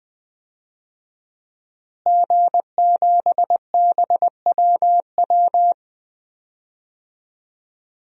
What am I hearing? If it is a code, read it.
Morse code, heard as a keyed tone, about 20 wpm, 700 Hz. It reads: G7BWW
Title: DXCC-100 Call Signs - 5 Characters - Repeated 2x - 20wpm